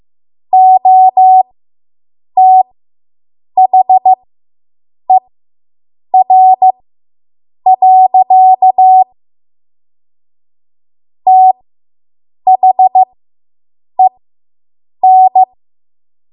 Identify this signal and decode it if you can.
Morse code, heard as a keyed tone, about 15 wpm, spaced as for 7 wpm, 750 Hz. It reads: OTHER. THEN